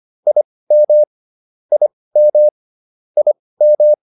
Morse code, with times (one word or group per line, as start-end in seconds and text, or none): 0.3-1.0 IM
1.7-2.5 IM
3.2-3.9 IM